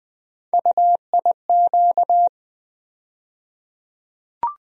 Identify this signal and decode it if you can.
Morse code, heard as a keyed tone, about 20 wpm, 700 Hz. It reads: UIQ
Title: USA Call Sign Postfixes - 3 Letters 20wpm